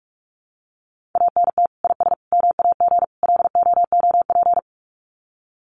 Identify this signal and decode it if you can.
Morse code, heard as a keyed tone, about 44 wpm, 700 Hz. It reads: ANT IS MAG LOOP
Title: Common QSO Phrases 45wpm